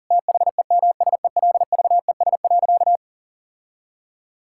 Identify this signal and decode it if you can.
Morse code, heard as a keyed tone, about 40 wpm, 700 Hz. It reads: THEMSELVES.
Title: Sentences from Top 1000 Words - Review Part 5 - 40wpm